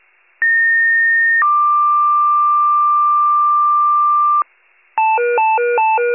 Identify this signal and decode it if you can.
Two-tone page, heard as a one-second tone, then a three-second tone, about 1850 then 1180 Hz. It is heard from 0.4 to 4.4 s.
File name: pager / 33 WCFM / wcalert